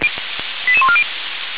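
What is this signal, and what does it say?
Five-tone selective call (ZVEI), heard as quick runs of five tones, tones of about 70 ms. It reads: 88140
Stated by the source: ZVEI five-tone selective call received on an SDR.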